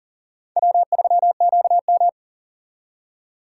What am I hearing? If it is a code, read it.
Morse code, heard as a keyed tone, about 52 wpm, 700 Hz. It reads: W3QM